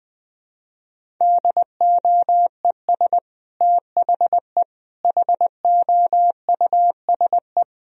Morse code, heard as a keyed tone, about 20 wpm, 700 Hz. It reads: DOES THE HOUSE
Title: Sentences from Top 1000 Words - Review Part 2 - 20wpm